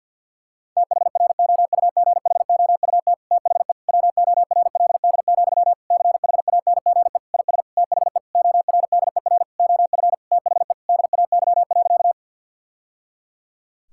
Morse code, heard as a keyed tone, about 52 wpm, 700 Hz. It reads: THROUGHOUT THE WORLD, CHANGE IS THE ORDER OF THE DAY.
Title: Sentences from Top 1000 Words - Review Part 5 - 50wpm